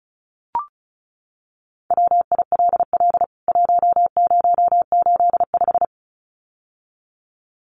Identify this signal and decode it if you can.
Morse code, heard as a keyed tone, about 34 wpm, 700 Hz. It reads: WILL 1085
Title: CWT Exchange 35wpm